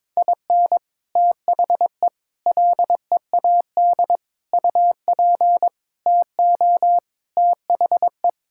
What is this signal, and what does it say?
Morse code, heard as a keyed tone, about 22 wpm, 700 Hz. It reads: IN THE LEAD UP TO THE
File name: practice / 22wpm / Sets of 6 Words - 22wpm